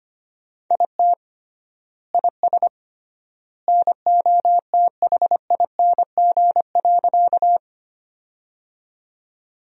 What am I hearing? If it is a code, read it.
Morse code, heard as a keyed tone, about 25 wpm, 700 Hz. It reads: IT IS NOTHING.
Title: 3x Word-Spacing - Sentences from Top 500 Words - 25wpm